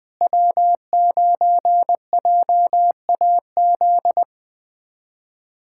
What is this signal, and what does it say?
Morse code, heard as a keyed tone, about 21 wpm, 700 Hz. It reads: W9JAZ
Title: USA Callsign DE Callsign - 5 Characters - 20wpm